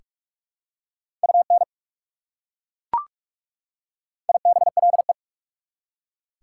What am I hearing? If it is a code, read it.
Morse code, heard as a keyed tone, about 41 wpm, 690 Hz. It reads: UN IBLE
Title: Prefixes and Suffixes 45wpm